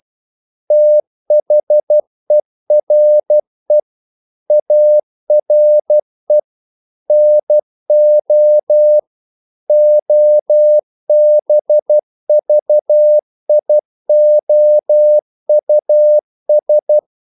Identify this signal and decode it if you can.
Morse code, heard as a keyed tone, about 12 wpm, 600 Hz. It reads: THERE ARE NO OBVIOUS
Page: English quotations in CW (telegraphy - morse)